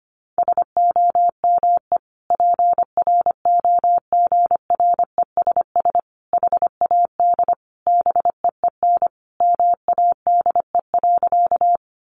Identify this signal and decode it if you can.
Morse code, heard as a keyed tone, about 25 wpm, 700 Hz. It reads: SOME PROGRESS HAD BEEN MADE.
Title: Sentences from Top 1400 Words 25wpm